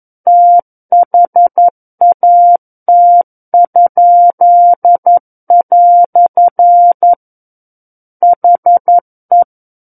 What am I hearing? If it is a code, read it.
Morse code, heard as a keyed tone, about 11 wpm, 700 Hz. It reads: THAT?" HE
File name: Book One / 02-Chapter 2 - 11 WPM